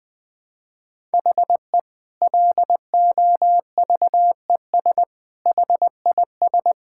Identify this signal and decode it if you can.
Morse code, heard as a keyed tone, about 19 wpm, 700 Hz. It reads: HE LOVES HIS
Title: Difficult Sentences with Hints - 20wpm